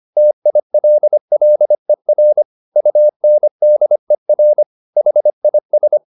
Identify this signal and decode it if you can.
Morse code, heard as a keyed tone, about 25 wpm, 600 Hz. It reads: TILLER UNDER HIS